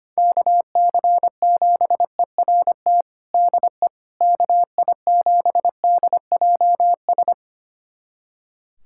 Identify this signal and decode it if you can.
Morse code, heard as a keyed tone, about 25 wpm, 700 Hz. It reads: KC7ERT DE KI7DJS
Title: USA Callsign DE Callsign - 6 Characters - 25wpm